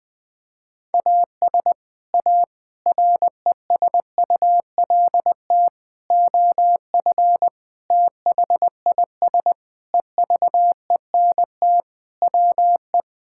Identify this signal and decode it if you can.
Morse code, heard as a keyed tone, about 20 wpm, 700 Hz. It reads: AS A RESULT OF THIS EVENT WE
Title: Sentences with 5-Word Phrases - Spoken Beginning - 20wpm